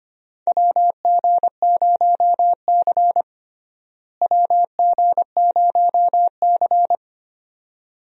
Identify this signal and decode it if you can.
Morse code, heard as a keyed tone, about 25 wpm, 700 Hz. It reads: WG0C WG0C
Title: USA Call Signs - 4 Characters - Repeated 2x - 25wpm